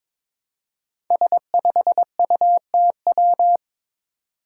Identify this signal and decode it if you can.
Morse code, heard as a keyed tone, about 22 wpm, 700 Hz. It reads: S5UTW
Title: Five Challenging Letter-Number Combinations 22wpm